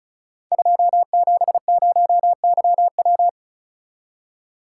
Morse code, heard as a keyed tone, about 34 wpm, 700 Hz. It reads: 270YW
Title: Code Groups - 5 Letter:Numbers - 35wpm